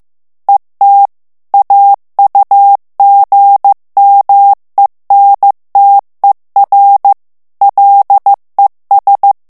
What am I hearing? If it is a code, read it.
Morse code, heard as a keyed tone, about 15 wpm, 800 Hz. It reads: ET AUGMENTER LES
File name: morse mp3 / FRANCAIS TEXTE CLAIR 1 VITESSE 14 MOTS MINUTES